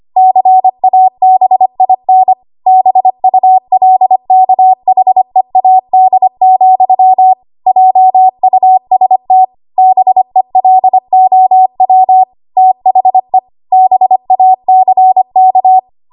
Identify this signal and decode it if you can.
Morse code, heard as a keyed tone, about 25 wpm, 750 Hz. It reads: CABIN BULKHEAD, JUST BELOW THE BACK
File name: Archive / 25 WPM / 25WPM